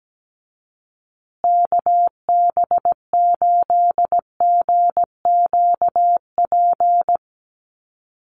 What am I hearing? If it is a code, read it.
Morse code, heard as a keyed tone, about 17 wpm, 700 Hz. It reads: KB8GQP
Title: USA Call Signs Repeated 3x 17wpm